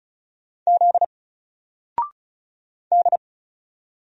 Morse code, heard as a keyed tone, about 36 wpm, 700 Hz. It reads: Z D
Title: Single Character (Letter:Number:Punctuation:Prosign) 35wpm